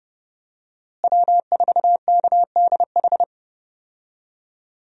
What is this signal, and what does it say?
Morse code, heard as a keyed tone, about 30 wpm, 700 Hz. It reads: W4KDH
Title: USA Call Signs - 5 Characters - Repeated 2x - 30wpm